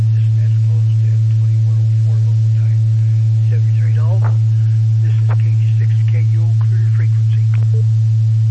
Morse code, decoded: none

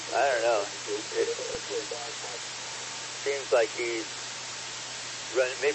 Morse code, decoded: none

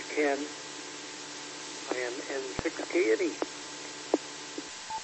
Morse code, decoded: none

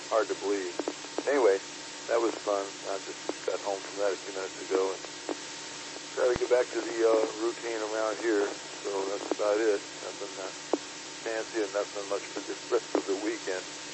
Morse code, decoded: none

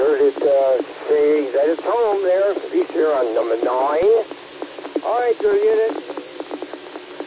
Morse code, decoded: none

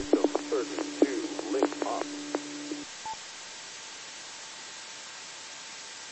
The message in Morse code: none

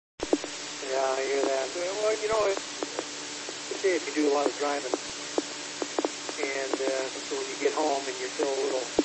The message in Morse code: none

